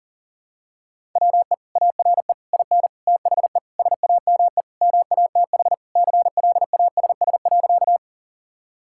Morse code, heard as a keyed tone, about 40 wpm, 700 Hz. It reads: WE ARE IN THE SAME MATH CLASS.
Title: Sentences from Top 1000 Words - Review Part 6 - 40wpm